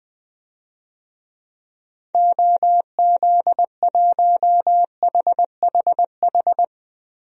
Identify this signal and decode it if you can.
Morse code, heard as a keyed tone, about 20 wpm, 700 Hz. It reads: OZ1HHH
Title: DXCC-100 Call Signs - 6 Characters - Repeated 2x - 20wpm